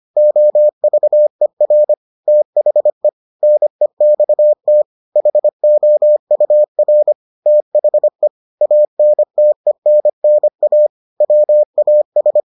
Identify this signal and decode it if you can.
Morse code, heard as a keyed tone, about 25 wpm, 600 Hz. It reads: OVER THE NEXT HOUR THE ANTENNA WAS